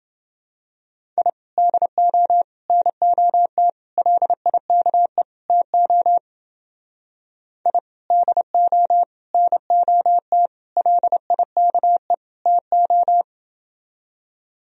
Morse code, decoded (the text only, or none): I DO NOT LIKE TO I DO NOT LIKE TO